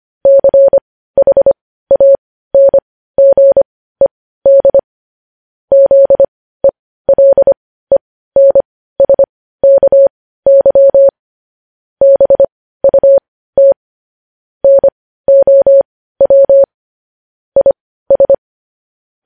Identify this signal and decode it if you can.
Morse code, heard as a keyed tone, about 25 wpm, spaced as for 15 wpm, 550 Hz. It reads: CHANGED ZELENSKY BUT NOW IS
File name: podcast / News headlines-1x-25x15